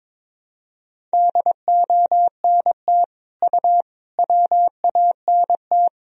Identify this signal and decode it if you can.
Morse code, heard as a keyed tone, about 22 wpm, 700 Hz. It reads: DONT U WANT
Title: Sentences from Top 300 Words and Abbreviations 22wpm